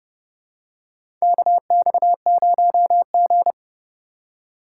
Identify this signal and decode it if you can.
Morse code, heard as a keyed tone, about 30 wpm, 700 Hz. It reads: KX0G
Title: USA Call Signs - 4 Characters - 2x Character Spacing - 30wpm